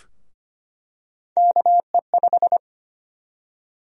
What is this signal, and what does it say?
Morse code, heard as a keyed tone, about 25 wpm, 700 Hz. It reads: KE5
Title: USA Call Sign Prefixes - 25wpm